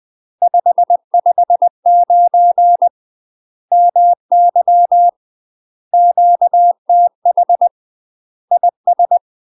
Morse code, heard as a keyed tone, about 20 wpm, 700 Hz. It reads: 559 MY QTH IS
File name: UK MASTER QSOs TRAINING / 20WPM / UK MASTER QSOs 0066